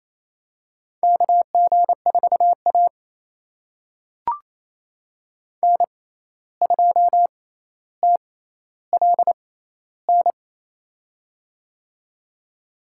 Morse code, heard as a keyed tone, about 27 wpm, spaced as for 10 wpm, 700 Hz. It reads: KG4AN 2TLN
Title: USA Call Signs - 6x Character Spacing - 28wpm